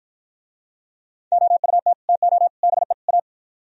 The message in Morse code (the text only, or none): OUT TO BE A